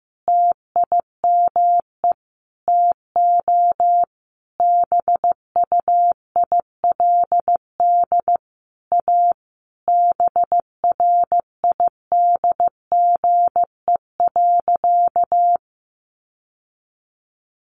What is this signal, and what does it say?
Morse code, heard as a keyed tone, about 15 wpm, 700 Hz. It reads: TIME TO BUILD A BRIDGE.
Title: Sentences from Top 1400 Words 15wpm